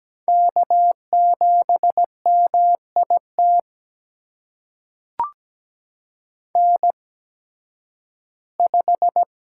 Morse code, heard as a keyed tone, about 17 wpm, spaced as for 11 wpm, 700 Hz. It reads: K7MIT N5
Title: USA Call Signs - 5 Characters - 8x Character Spacing - 17wpm